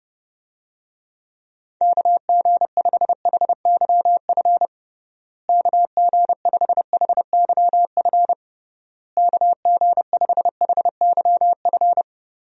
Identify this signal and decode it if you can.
Morse code, heard as a keyed tone, about 30 wpm, 700 Hz. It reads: KG5HYF KG5HYF KG5HYF
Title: USA Call Signs - 6 Characters - Repeated 3x - 30wpm